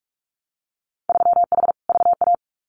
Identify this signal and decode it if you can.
Morse code, heard as a keyed tone, about 46 wpm, 700 Hz. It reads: 3H VA
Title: Winter Field Day Exchanges Encore 45wpm